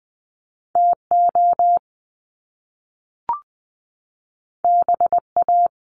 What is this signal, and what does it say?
Morse code, heard as a keyed tone, about 20 wpm, 700 Hz. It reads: TO BA
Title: Top Two Letter Combinations in Words 20wpm